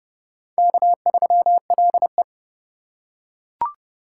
Morse code, heard as a keyed tone, about 29 wpm, 700 Hz. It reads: K3LE